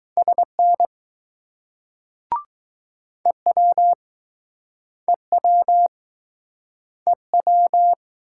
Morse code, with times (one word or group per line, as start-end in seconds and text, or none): 0.2-0.9 SN
3.3-3.9 EW
5.1-5.9 EW
7.1-7.9 EW